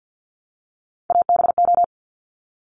Morse code, heard as a keyed tone, about 50 wpm, 690 Hz. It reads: ABO
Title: Top Three Letter Combinations in Words 50wpm